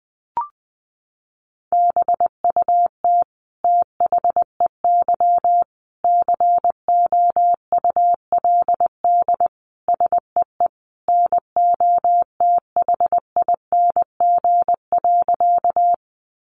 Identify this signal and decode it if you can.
Morse code, heard as a keyed tone, about 20 wpm, 700 Hz. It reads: BUT THEY COULD SEE NOTHING.
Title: Sentences from Top 500 Words 20wpm